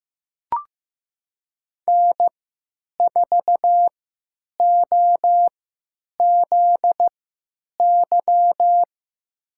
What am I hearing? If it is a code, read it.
Morse code, heard as a keyed tone, about 15 wpm, spaced as for 8 wpm, 700 Hz. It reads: N4OZY
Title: USA Call Signs - 5 Characters - 3x Character Spacing - 15wpm